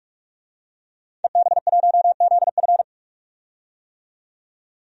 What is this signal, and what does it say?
Morse code, heard as a keyed tone, about 41 wpm, 700 Hz. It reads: EB1ZF
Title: DXCC-100 Call Signs - 5 Characters - 45wpm